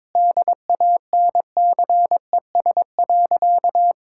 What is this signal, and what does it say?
Morse code, heard as a keyed tone, about 22 wpm, 700 Hz. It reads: DANCES.
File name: practice / 22wpm / Sentences with 3-Word Phrases - 22wpm